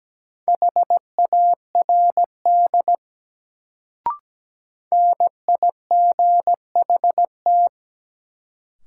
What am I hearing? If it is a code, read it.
Morse code, heard as a keyed tone, about 17 wpm, 700 Hz. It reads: HARD NIGHT
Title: Sentences from Top 300 Words - New Word Drill - 17wpm